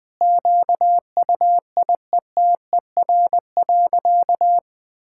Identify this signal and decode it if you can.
Morse code, heard as a keyed tone, about 21 wpm, 700 Hz. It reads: QUIETER.
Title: Sentences with 5-Word Phrases - Spoken Beginning - 20wpm